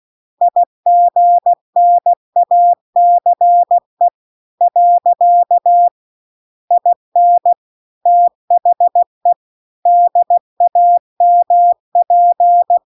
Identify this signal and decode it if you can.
Morse code, heard as a keyed tone, about 16 wpm, 700 Hz. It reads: IGNACE . IN THE DAMP